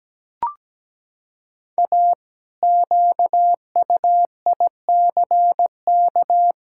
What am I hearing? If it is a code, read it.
Morse code, heard as a keyed tone, about 17 wpm, 700 Hz. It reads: A QUICK
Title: Pangram Challenge v2 - 17wpm